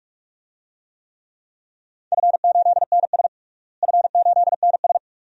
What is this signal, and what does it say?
Morse code, heard as a keyed tone, about 41 wpm, 700 Hz. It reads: F8NS F8NS
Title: DXCC-100 Call Signs - 4 Characters - Repeated 2x - 45wpm